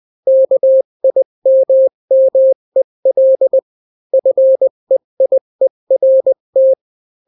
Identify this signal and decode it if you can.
Morse code, heard as a keyed tone, about 20 wpm, 530 Hz. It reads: KIMMEL FEIERT
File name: daily / 18wpm